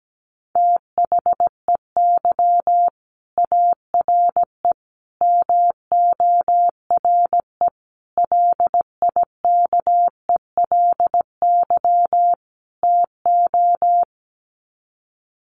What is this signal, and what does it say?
Morse code, heard as a keyed tone, about 17 wpm, 700 Hz. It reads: THEY ARE MORE LIKELY TO
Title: Sets of 5 Words Encore2 17wpm